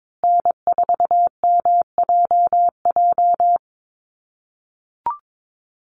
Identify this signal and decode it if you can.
Morse code, heard as a keyed tone, about 22 wpm, 700 Hz. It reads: N4MJJ
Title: USA Call Signs Repeated 3x 22wpm